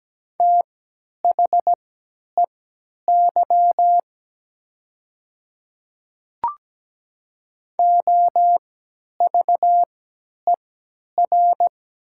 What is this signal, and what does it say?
Morse code, heard as a keyed tone, about 17 wpm, spaced as for 10 wpm, 700 Hz. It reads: THEY OVER
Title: 3x Character Spacing - Top 100 Words - 17wpm